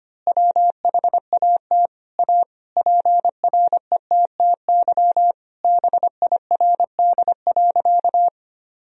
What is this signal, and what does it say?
Morse code, heard as a keyed tone, about 25 wpm, 700 Hz. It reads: WHAT A PRETTY BIRD.
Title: Sentences from Top 1000 Words - Review Part 2 - 25wpm